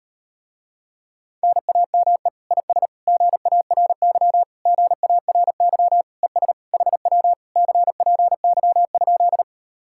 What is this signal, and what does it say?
Morse code, heard as a keyed tone, about 39 wpm, 700 Hz. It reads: NAME IS GARY GARY ES HW CPY?